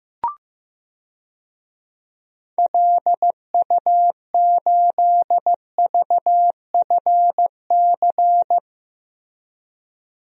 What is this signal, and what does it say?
Morse code, heard as a keyed tone, about 15 wpm, 700 Hz. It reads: LU8VFC
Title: DXCC-100 Call Signs - 6 Characters - Repeated 3x - 15wpm